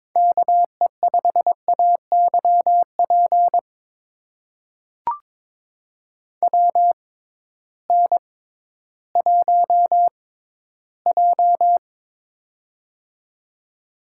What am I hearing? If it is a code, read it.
Morse code, heard as a keyed tone, about 21 wpm, spaced as for 13 wpm, 700 Hz. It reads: KE5AYP WN1J
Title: USA Call Signs - 6x Character Spacing - 22wpm